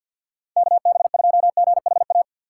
Morse code, heard as a keyed tone, about 51 wpm, 700 Hz. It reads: KB2GHA
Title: USA Call Signs - 6 Characters - 4x Character Spacing - 50wpm